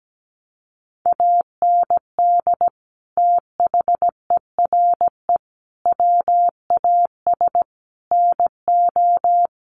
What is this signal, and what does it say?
Morse code, heard as a keyed tone, about 17 wpm, 700 Hz. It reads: AND THERE WAS NO
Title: Sentences from Top 1500 Words 17wpm